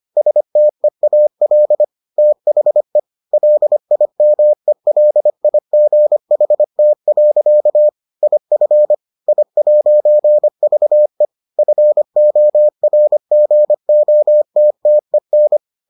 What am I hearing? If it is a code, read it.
Morse code, heard as a keyed tone, about 25 wpm, 600 Hz. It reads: STEAL THE LIMELIGHT. IF I'VE FORGOTTEN